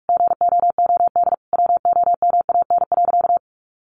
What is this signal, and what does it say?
Morse code, heard as a keyed tone, about 44 wpm, 700 Hz. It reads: GOOD WOMAN.
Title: Sentences from Top 1100 Words 45wpm